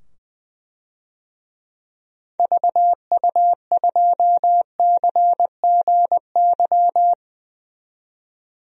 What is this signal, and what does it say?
Morse code, heard as a keyed tone, about 20 wpm, 700 Hz. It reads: VU2CGY
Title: DXCC-100 Call Signs - 6 Characters - 20wpm